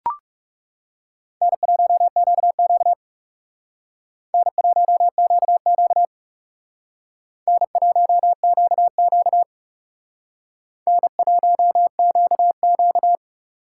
Morse code, heard as a keyed tone, about 38 wpm, 700 Hz. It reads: N1QQ N1QQ N1QQ N1QQ